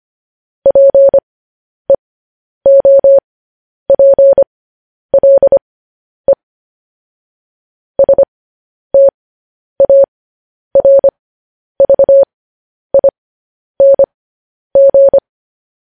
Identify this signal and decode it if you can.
Morse code, heard as a keyed tone, about 25 wpm, spaced as for 10 wpm, 550 Hz. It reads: PEOPLE STARVING